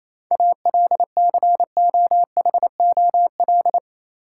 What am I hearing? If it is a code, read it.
Morse code, heard as a keyed tone, about 28 wpm, 700 Hz. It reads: ALCOHOL
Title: Sentences from Top 1400 Words - New Word Drill Alt - 28wpm